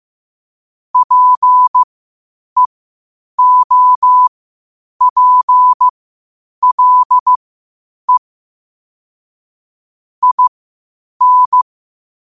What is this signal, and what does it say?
Morse code, heard as a keyed tone, about 15 wpm, spaced as for 8 wpm, 1000 Hz. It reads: PEOPLE IN
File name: qotdfiles / 8wpm